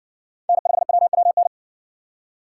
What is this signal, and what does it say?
Morse code, heard as a keyed tone, about 60 wpm, 700 Hz. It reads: N5WWN